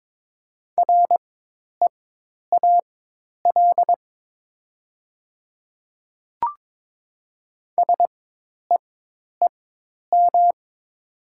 Morse code, heard as a keyed tone, about 22 wpm, spaced as for 10 wpm, 700 Hz. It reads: REAL SEEM
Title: 4x Character Spacing - Top 300 Words - 22wpm